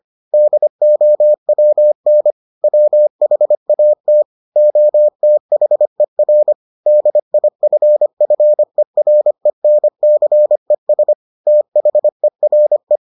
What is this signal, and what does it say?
Morse code, heard as a keyed tone, about 25 wpm, 600 Hz. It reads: DOWN WHAT OTHER DIFFERENCES THERE